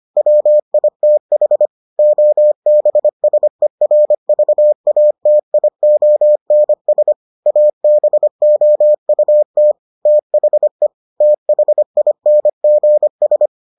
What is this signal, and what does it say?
Morse code, heard as a keyed tone, about 25 wpm, 600 Hz. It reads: WITH OBSERVATIONS ABOUT THE THINGS